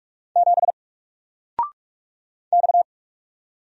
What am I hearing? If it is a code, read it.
Morse code, heard as a keyed tone, about 45 wpm, 700 Hz. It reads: 7 X
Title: Single Letter-Number 45wpm